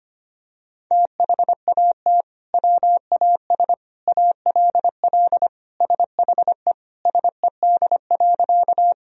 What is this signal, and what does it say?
Morse code, heard as a keyed tone, about 25 wpm, 700 Hz. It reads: THAT WAS ALL SHE SED.
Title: Sentences from Top 100 Words and Abbreviations 25wpm